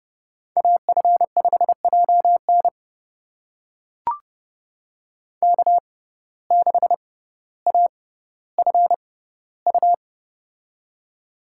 Morse code, heard as a keyed tone, about 30 wpm, spaced as for 15 wpm, 700 Hz. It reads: AF5JN K6AFU
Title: USA Call Signs - 5 Characters - 6x Character Spacing - 30wpm